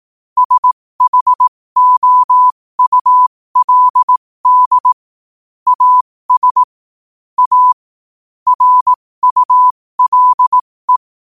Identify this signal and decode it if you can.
Morse code, heard as a keyed tone, about 18 wpm, spaced as for 15 wpm, 1000 Hz. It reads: SHOULD AS A RULE
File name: qotdfiles / 15wpm